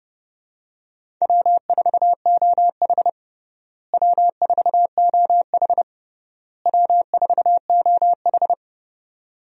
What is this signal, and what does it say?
Morse code, heard as a keyed tone, about 32 wpm, 700 Hz. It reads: W4OH W4OH W4OH
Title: USA Call Signs - 4 Characters - Repeated 3x - 30wpm